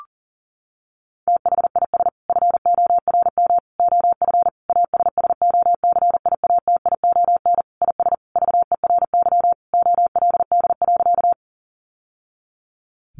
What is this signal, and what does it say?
Morse code, heard as a keyed tone, about 40 wpm, 700 Hz. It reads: THIS FORM OF ASSOCIATION IS VERY OLD.
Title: Sentences from Top 1100 Words 40wpm